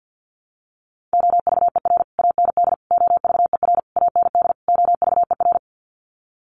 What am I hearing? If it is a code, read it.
Morse code, heard as a keyed tone, about 50 wpm, 700 Hz. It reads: OVER AND OVER AND OVER
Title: Sets of 5 Words Encore 50wpm